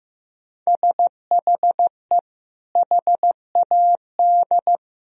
Morse code, heard as a keyed tone, about 16 wpm, 700 Hz. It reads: SHE HAD
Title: Sentences from Top 1000 Words - Review Part 2 - 15wpm